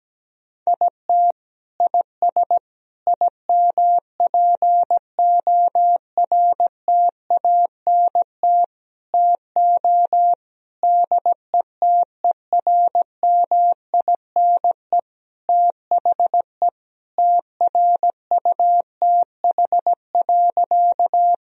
Morse code, heard as a keyed tone, about 17 wpm, 700 Hz. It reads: IT IS IMPORTANT TO DETERMINE THE TRUTH.
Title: Sentences from Top 1000 Words - Review Part 4 - 17wpm